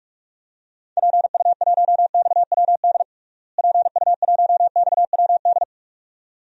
Morse code, heard as a keyed tone, about 42 wpm, 700 Hz. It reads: PU1XWD PU1XWD
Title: DXCC-100 Call Signs - 6 Characters - Repeated 2x - 45wpm